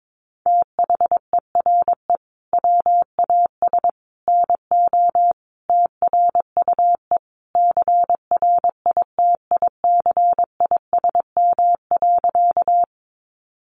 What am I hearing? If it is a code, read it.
Morse code, heard as a keyed tone, about 22 wpm, 700 Hz. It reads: THERE WAS NO TRUE CRITICISM.
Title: Sentences from Top 1300 Words 22wpm